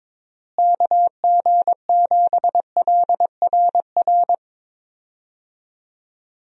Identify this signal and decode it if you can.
Morse code, heard as a keyed tone, about 22 wpm, 700 Hz. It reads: KG7LRR
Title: USA Call Signs - 6 Characters - Repeated 3x - 22wpm